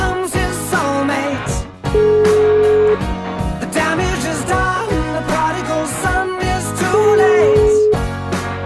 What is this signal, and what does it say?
Ringback tone (Europe, 425 Hz), with noise.